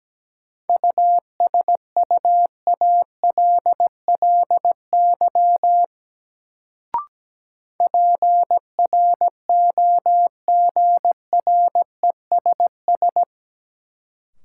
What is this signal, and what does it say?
Morse code, heard as a keyed tone, about 17 wpm, 700 Hz. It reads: USUALLY PROGRESS
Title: Sentences from Top 1400 Words - New Word Drill Alt - 17wpm